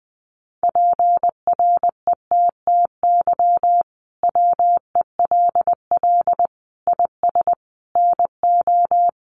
Morse code, heard as a keyed tone, about 20 wpm, 700 Hz. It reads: PRETTY WELL IS NO
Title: Sentences from Top 1000 Words 20wpm